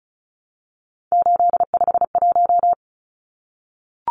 Morse code, heard as a keyed tone, about 34 wpm, 700 Hz. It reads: 851
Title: Numbers - 3-Digit 35wpm